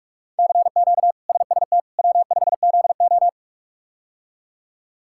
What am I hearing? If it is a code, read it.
Morse code, heard as a keyed tone, about 46 wpm, 700 Hz. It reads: CQ SST W5ZO
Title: K1USN SST QSO Practice - 45wpm